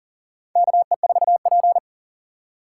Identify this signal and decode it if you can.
Morse code, heard as a keyed tone, about 37 wpm, 700 Hz. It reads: KE4P
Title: USA Call Signs - 4 Characters - 6x Character Spacing - 40wpm